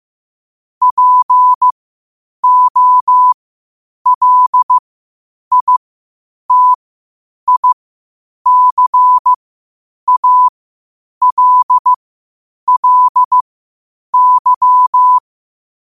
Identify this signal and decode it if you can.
Morse code, heard as a keyed tone, about 15 wpm, spaced as for 8 wpm, 1000 Hz. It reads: POLITICALLY
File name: qotdfiles / 8wpm